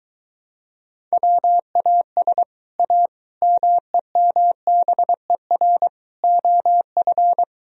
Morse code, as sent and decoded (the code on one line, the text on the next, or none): .-- .- ... / .- / -- . -- -... .-. / --- ..-.
WAS A MEMBER OF